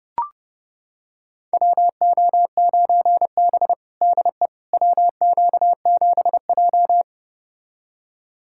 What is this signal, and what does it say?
Morse code, heard as a keyed tone, about 30 wpm, 700 Hz. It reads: WO9B DE WQ7J
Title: USA Callsign DE Callsign - 4 Characters - 30wpm